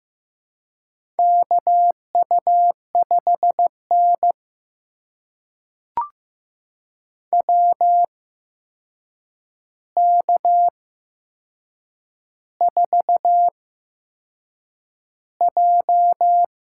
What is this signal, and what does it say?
Morse code, heard as a keyed tone, about 15 wpm, spaced as for 6 wpm, 700 Hz. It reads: KU5N WK4J
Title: USA Call Signs - 4 Characters - 8x Character Spacing - 15wpm